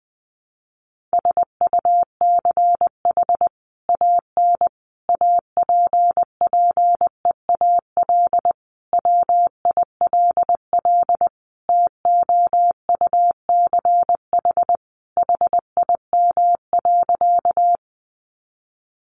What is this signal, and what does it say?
Morse code, heard as a keyed tone, about 20 wpm, 700 Hz. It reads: SUCH AN APPEAL WILL TOUCH HIM.
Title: Sentences from Top 1300 Words 20wpm